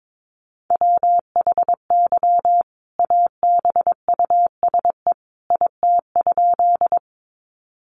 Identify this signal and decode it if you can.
Morse code, heard as a keyed tone, about 22 wpm, 700 Hz. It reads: WHY ABUSE IT?
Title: Sentences from Top 1300 Words 22wpm